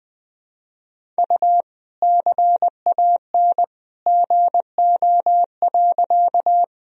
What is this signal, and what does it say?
Morse code, heard as a keyed tone, about 20 wpm, 700 Hz. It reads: U CAN GO.
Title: Sentences from Top 100 Words and Abbreviations 20wpm